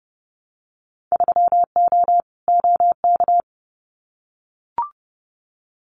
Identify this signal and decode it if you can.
Morse code, heard as a keyed tone, about 30 wpm, 700 Hz. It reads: 3O OK
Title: Winter Field Day Exchanges 30wpm